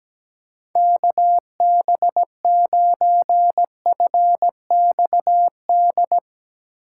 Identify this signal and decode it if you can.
Morse code, heard as a keyed tone, about 17 wpm, 700 Hz. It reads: KB9FXD
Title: USA Call Signs - 3x Character Spacing - 17wpm